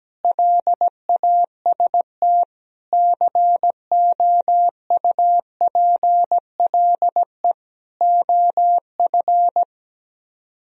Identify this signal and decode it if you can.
Morse code, heard as a keyed tone, about 17 wpm, 700 Hz. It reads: LAST COUPLE OF
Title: Speed-Racing - Sets of 5 Words 17wpm